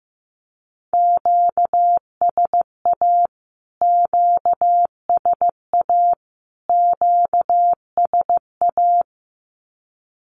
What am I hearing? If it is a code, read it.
Morse code, heard as a keyed tone, about 15 wpm, 700 Hz. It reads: QSA QSA QSA
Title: Q-Codes Repeated 3x 15wpm